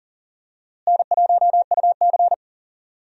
Morse code, heard as a keyed tone, about 39 wpm, 700 Hz. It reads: N1UC
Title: USA Call Signs - 4 Characters - Repeated 2x - 40wpm